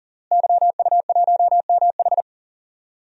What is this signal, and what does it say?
Morse code, heard as a keyed tone, about 37 wpm, 700 Hz. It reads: YU1MH
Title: DXCC-100 Call Signs - 5 Characters - Repeated 3x - 40wpm